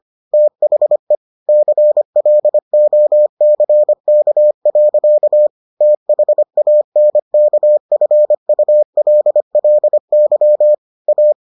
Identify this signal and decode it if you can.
Morse code, heard as a keyed tone, about 25 wpm, 600 Hz. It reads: THE CLOCK. THANKFULLY A